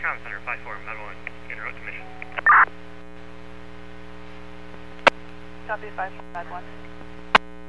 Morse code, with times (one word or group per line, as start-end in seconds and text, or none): none